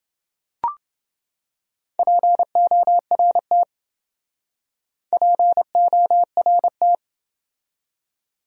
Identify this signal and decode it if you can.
Morse code, heard as a keyed tone, about 28 wpm, 700 Hz. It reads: PORT PORT